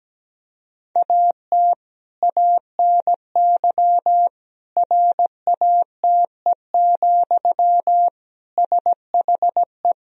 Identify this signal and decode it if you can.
Morse code, heard as a keyed tone, about 17 wpm, 700 Hz. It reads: AT ANY RATE, SHE
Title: Sentences from Top 1000 Words - Review Part 4 - 17wpm